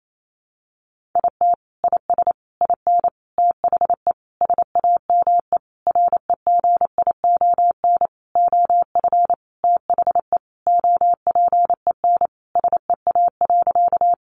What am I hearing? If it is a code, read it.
Morse code, heard as a keyed tone, about 28 wpm, 700 Hz. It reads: IT IS IN THE SAME REGION OF THE OPEN SEA.
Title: Sentences from Top 1500 Words 28wpm